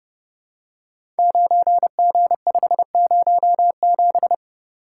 Morse code, heard as a keyed tone, about 29 wpm, 700 Hz. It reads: 9G507
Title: Code Groups - 5 Letter:Numbers - 30wpm